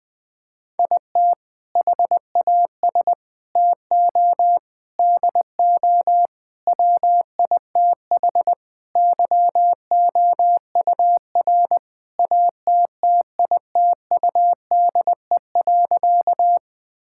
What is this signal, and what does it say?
Morse code, heard as a keyed tone, about 20 wpm, 700 Hz. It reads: IT HAS TO DO WITH YOUR ATTITUDE.
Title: Sentences with 5-Word Phrases - 20wpm